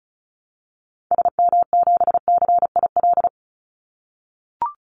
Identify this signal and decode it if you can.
Morse code, heard as a keyed tone, about 36 wpm, 700 Hz. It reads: SM7CIL
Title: Call Signs - 35wpm